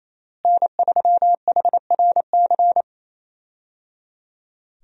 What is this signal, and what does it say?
Morse code, heard as a keyed tone, about 28 wpm, 700 Hz. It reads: N3HRC
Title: USA Call Signs - 5 Characters - Repeated 3x - 28wpm